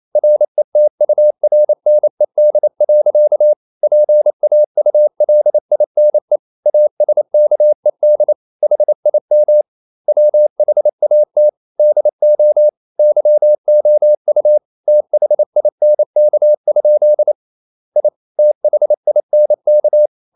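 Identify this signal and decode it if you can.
Morse code, heard as a keyed tone, about 28 wpm, 600 Hz. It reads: RETURNED. PAULINE ASKED HIM WHAT DO YOU THINK? I THINK
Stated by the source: Morse code practice at 28 wpm, text of Ken Follett's novel Never in English, from CW med Gnister.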